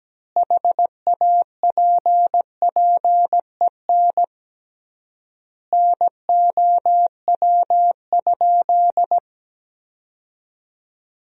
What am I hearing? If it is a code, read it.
Morse code, heard as a keyed tone, about 17 wpm, 700 Hz. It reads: HAPPEN NOW?
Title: 3x Word-Spacing - Sentences from Top 400 Words - 17wpm